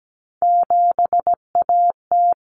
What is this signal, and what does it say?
Morse code, heard as a keyed tone, about 17 wpm, 700 Hz. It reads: 7AT
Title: Common Cut Numbers - 3-Digit 17wpm